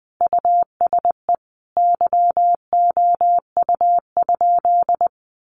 Morse code, heard as a keyed tone, about 19 wpm, 700 Hz. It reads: USE YOU?
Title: Sentences from Top 200 Words 20wpm